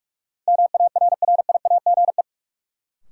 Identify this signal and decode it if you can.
Morse code, heard as a keyed tone, about 45 wpm, 700 Hz. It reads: MARRIAGE
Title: Sentences from Top 800 Words - New Word Drill - 45wpm